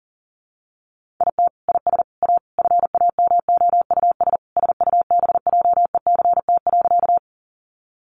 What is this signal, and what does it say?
Morse code, heard as a keyed tone, about 40 wpm, 700 Hz. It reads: IT IS A FAMOUS SUBJECT.
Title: Sentences from Top 1500 Words 40wpm